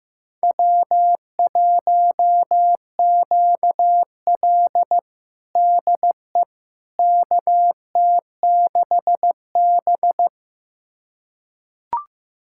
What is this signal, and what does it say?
Morse code, heard as a keyed tone, about 16 wpm, 700 Hz. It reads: W1QL DE KT6B
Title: USA Callsign DE Callsign - 4 Characters - 15wpm